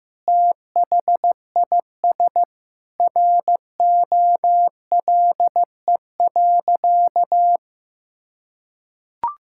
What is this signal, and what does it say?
Morse code, heard as a keyed tone, about 15 wpm, 700 Hz. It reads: THIS ROLE.
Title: Sentences from Top 1000 Words - Review Part 6 - 15wpm